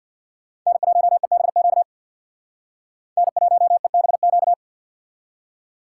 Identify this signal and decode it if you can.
Morse code, heard as a keyed tone, about 55 wpm, 700 Hz. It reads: N1EBQ N1EBQ
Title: Speed Racing - USA 1x3 Call Signs 50wpm